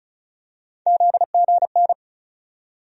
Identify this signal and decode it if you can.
Morse code, heard as a keyed tone, about 34 wpm, 700 Hz. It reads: ZGN
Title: USA Call Sign Postfixes - 3 Letters 35wpm